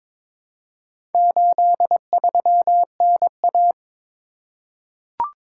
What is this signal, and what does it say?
Morse code, heard as a keyed tone, about 22 wpm, 700 Hz. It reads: 83NA
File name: practice / 22wpm / Four Letter-Number Combinations - 22wpm